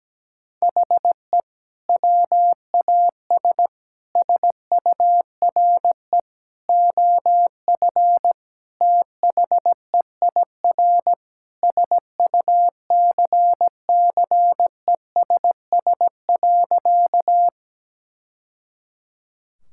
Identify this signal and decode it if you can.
Morse code, heard as a keyed tone, about 17 wpm, 700 Hz. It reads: HE WAS SURE OF THEIR SUCCESS.
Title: Sentences from Top 1000 Words - Review Part 6 - 17wpm